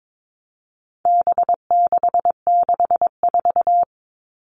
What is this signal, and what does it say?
Morse code, heard as a keyed tone, about 22 wpm, 700 Hz. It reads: B664
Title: All Cut Numbers - 4-Digit 22wpm